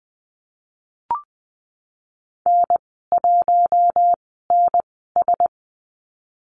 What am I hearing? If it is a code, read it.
Morse code, heard as a keyed tone, about 20 wpm, spaced as for 15 wpm, 700 Hz. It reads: N1NS
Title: 048 Lesson 9g - Number 4 - Call Signs 20wpm